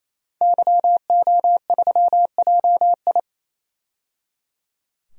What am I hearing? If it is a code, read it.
Morse code, heard as a keyed tone, about 28 wpm, 700 Hz. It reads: YO3JI
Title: DXCC-100 Call Signs - 5 Characters - 28wpm